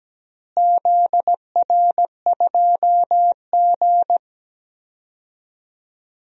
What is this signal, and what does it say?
Morse code, heard as a keyed tone, about 17 wpm, 700 Hz. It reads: ZR2G
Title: DXCC-100 Call Signs - 4 Characters - Repeated 3x - 17wpm